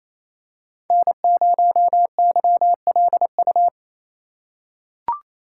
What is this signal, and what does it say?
Morse code, heard as a keyed tone, about 28 wpm, 700 Hz. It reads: N0YLU
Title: USA Call Signs - 5 Characters - 4x Character Spacing - 28wpm